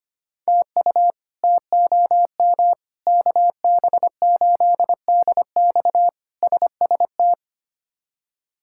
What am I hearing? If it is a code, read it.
Morse code, heard as a keyed tone, about 25 wpm, 700 Hz. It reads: TU TOM KB8DX SST
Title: K1USN SST QSO Practice - 25wpm